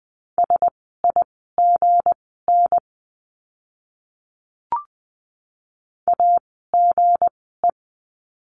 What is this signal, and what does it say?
Morse code, heard as a keyed tone, about 20 wpm, spaced as for 14 wpm, 700 Hz. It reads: SIGN AGE
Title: 160 Lesson 26f - Letter g - Words 20wpm